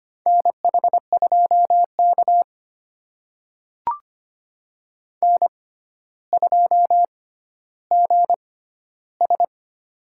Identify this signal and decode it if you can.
Morse code, heard as a keyed tone, about 25 wpm, spaced as for 13 wpm, 700 Hz. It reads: NH2K N2GS
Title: USA Call Signs - 4 Characters - 6x Character Spacing - 25wpm